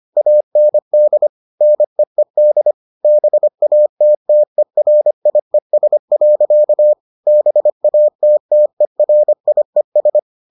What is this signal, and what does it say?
Morse code, heard as a keyed tone, about 25 wpm, 600 Hz. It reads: AND NEED BATTERIES. BATTERIES